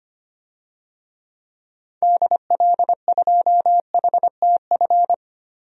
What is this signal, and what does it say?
Morse code, heard as a keyed tone, about 25 wpm, 700 Hz. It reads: DL2HTF
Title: DXCC-100 Call Signs - 6 Characters - 25wpm